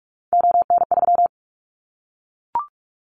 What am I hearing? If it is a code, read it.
Morse code, heard as a keyed tone, about 44 wpm, 700 Hz. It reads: ON3
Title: Callsign Prefixes for Top 25 DXCC - 45wpm